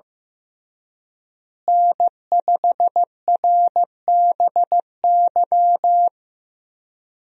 Morse code, heard as a keyed tone, about 15 wpm, 700 Hz. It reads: N5RBY
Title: USA Call Signs - 5 Characters - Repeated 3x - 15wpm